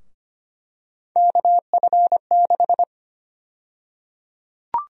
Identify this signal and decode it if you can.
Morse code, heard as a keyed tone, about 25 wpm, 700 Hz. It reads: KF6
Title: USA Call Sign Prefixes - 25wpm